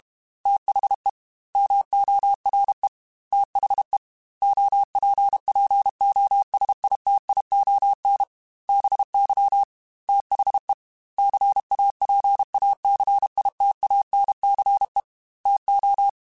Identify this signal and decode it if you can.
Morse code, heard as a keyed tone, about 32 wpm, 800 Hz. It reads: THE MORE THE OPPOSITION BY THE CAPACITANCE TO